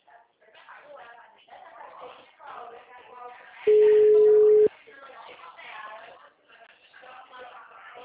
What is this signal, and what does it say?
Ringback tone (Europe, 425 Hz), with speech.